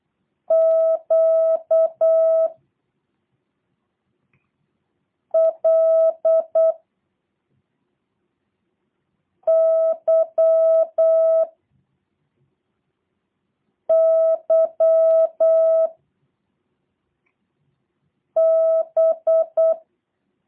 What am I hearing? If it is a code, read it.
Morse code, heard as a keyed tone, about 8 wpm, 660 Hz. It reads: Q L Y Y B
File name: jh8cbh / 2023cw / review1-6